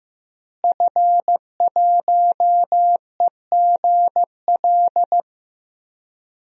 Morse code, heard as a keyed tone, about 15 wpm, 700 Hz. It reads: F1EGL
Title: DXCC-100 Call Signs - 5 Characters - Repeated 2x - 15wpm